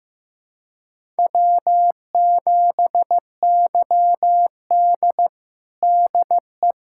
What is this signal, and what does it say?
Morse code, heard as a keyed tone, about 15 wpm, 700 Hz. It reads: W7YD DE
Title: USA Callsign DE Callsign - 4 Characters - 15wpm